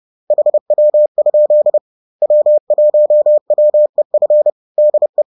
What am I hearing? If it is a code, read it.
Morse code, heard as a keyed tone, about 30 wpm, 600 Hz. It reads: HW? W1WEF DE
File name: 30wpm